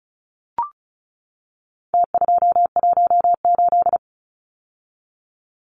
Morse code, heard as a keyed tone, about 34 wpm, 700 Hz. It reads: T218
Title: Numbers - Contest Style - 4-Digit 35wpm